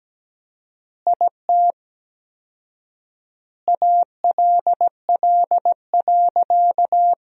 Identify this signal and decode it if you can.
Morse code, heard as a keyed tone, about 17 wpm, 700 Hz. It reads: IT ALL.
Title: 4x Word-Spacing - Sentences from Top 300 Words - 17wpm